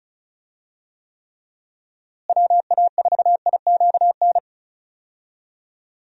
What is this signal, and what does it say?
Morse code, heard as a keyed tone, about 34 wpm, 700 Hz. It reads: WA4IQN
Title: USA Call Signs - 6 Characters - Repeated 3x - 35wpm